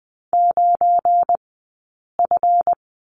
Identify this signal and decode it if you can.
Morse code, heard as a keyed tone, about 20 wpm, spaced as for 8 wpm, 700 Hz. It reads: 9F
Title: 171 Lesson 28d - Number 9 - Two Characters 20wpm